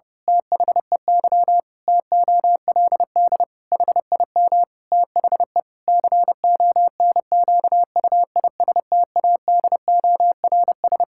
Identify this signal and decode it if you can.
Morse code, heard as a keyed tone, about 30 wpm, 700 Hz. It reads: THEY TOLD HIM THE CONQUISTADORS
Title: Pangram Challenge v2 - 30wpm